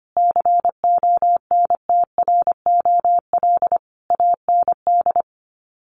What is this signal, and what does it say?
Morse code, heard as a keyed tone, about 25 wpm, 700 Hz. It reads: CONTROL AND
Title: Sets of 5 Words Encore2 25wpm